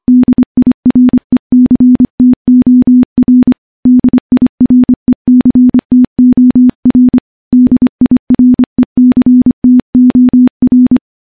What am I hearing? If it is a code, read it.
Morse code, heard as a keyed tone, about 25 wpm, 250 Hz. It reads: DIRECTOR DIRECTOR DIRECTOR
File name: morse-training-25WPM